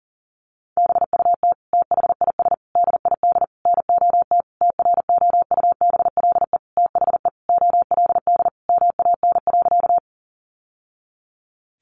Morse code, heard as a keyed tone, about 37 wpm, 700 Hz. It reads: BUT THIS DID NOT TROUBLE THE OLD MAN.
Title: Sentences from Top 1200 Words 40wpm